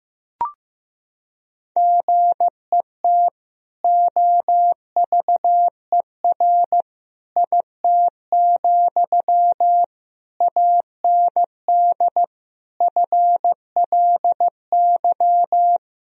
Morse code, heard as a keyed tone, about 16 wpm, 700 Hz. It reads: GET OVER IT, AND FLY
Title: Sentences from Top 1000 Words - Review Part 2 - 15wpm